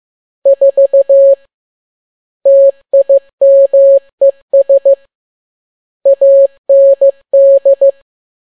Morse code, heard as a keyed tone, about 15 wpm, 550 Hz. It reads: 4 TIMES AND
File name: n6tt p1 15C13